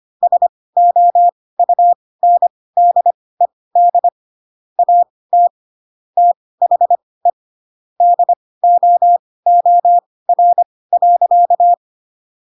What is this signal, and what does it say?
Morse code, heard as a keyed tone, about 25 wpm, spaced as for 18 wpm, 700 Hz. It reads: SOUNDED AT THE DOOR.
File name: Pinocchio / Pinocchio ch6 12 25 ik1hge